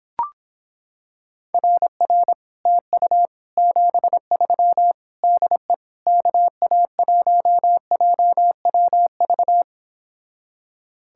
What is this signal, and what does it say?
Morse code, heard as a keyed tone, about 26 wpm, 700 Hz. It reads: RR TU 73 DE KA1JWV